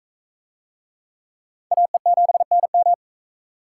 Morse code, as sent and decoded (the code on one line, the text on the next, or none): .- . --... -. --
AE7NM